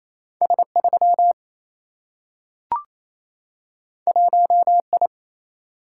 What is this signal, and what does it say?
Morse code, heard as a keyed tone, about 28 wpm, 700 Hz. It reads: S 3 1 I
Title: Single-Letter Single-Number 28wpm